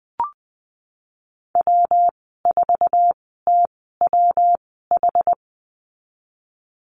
Morse code, heard as a keyed tone, about 20 wpm, spaced as for 15 wpm, 700 Hz. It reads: W4TWH
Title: 186 Lesson 30g - Letter b - Call Signs 20wpm